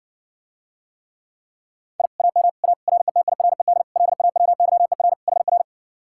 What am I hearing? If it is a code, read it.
Morse code, heard as a keyed tone, about 60 wpm, 700 Hz. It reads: I AM A RETIRED LAWYER HR